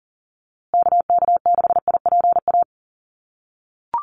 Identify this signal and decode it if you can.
Morse code, heard as a keyed tone, about 40 wpm, 700 Hz. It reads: KK6IPA
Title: USA Call Signs Repeated 2x 40wpm